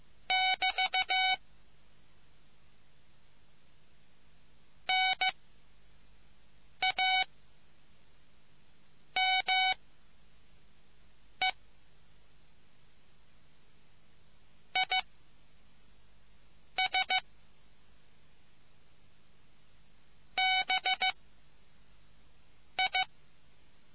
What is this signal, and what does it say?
Morse code, heard as a keyed tone, about 15 wpm, 750 Hz. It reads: <BT> N A M E I S B I